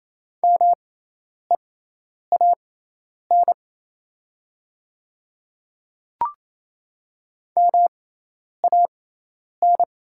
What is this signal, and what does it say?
Morse code, heard as a keyed tone, about 28 wpm, spaced as for 10 wpm, 700 Hz. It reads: MEAN MAN
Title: 6x Character Spacing - Top 200 Words - 28wpm